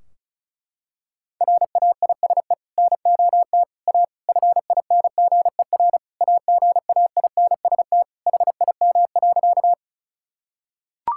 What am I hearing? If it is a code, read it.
Morse code, heard as a keyed tone, about 35 wpm, 700 Hz. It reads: RAISE NOT A FINGER AGAINST HIM.
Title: Sentences from Top 1600 Words 35wpm